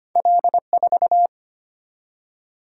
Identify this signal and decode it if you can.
Morse code, heard as a keyed tone, about 25 wpm, 700 Hz. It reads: L4
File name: Two Letter-Number Combinations - 25wpm